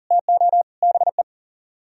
Morse code, heard as a keyed tone, about 47 wpm, 700 Hz. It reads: TO BE
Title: Speed-Racing - Sets of 5 Words 40wpm